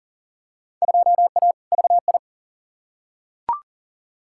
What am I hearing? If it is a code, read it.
Morse code, heard as a keyed tone, about 40 wpm, 700 Hz. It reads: 2A VI
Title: Field Day Exchanges - 40wpm